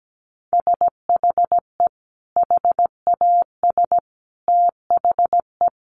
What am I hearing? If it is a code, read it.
Morse code, heard as a keyed tone, about 17 wpm, 700 Hz. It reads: SHE HAS THE
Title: Sentences from Top 900 Words 17wpm